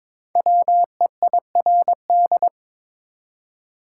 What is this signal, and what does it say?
Morse code, heard as a keyed tone, about 22 wpm, 690 Hz. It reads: WEIRD
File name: 5 Letter Words - First 3 Letters Given - 22wpm